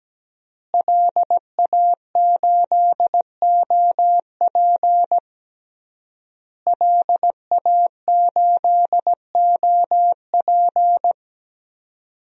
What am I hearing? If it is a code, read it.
Morse code, heard as a keyed tone, about 17 wpm, 700 Hz. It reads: LA8OP LA8OP
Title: DXCC-100 Call Signs - 5 Characters - Repeated 3x - 17wpm